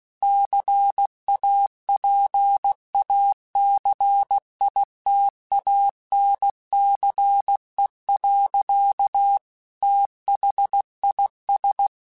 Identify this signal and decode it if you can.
Morse code, heard as a keyed tone, about 16 wpm, 800 Hz. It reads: CAPACITANCE. THIS